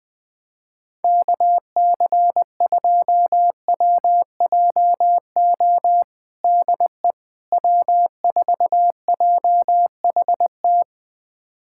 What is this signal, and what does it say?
Morse code, heard as a keyed tone, about 20 wpm, 700 Hz. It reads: KC2WJO DE W4JHT
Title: USA Callsign DE Callsign 20wpm